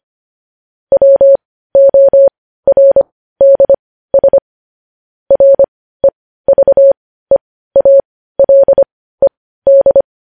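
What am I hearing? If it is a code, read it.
Morse code, heard as a keyed tone, about 25 wpm, spaced as for 15 wpm, 550 Hz. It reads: WORDS REVEALED
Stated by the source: Morse news headlines practice, possibly sent at 15 wpm.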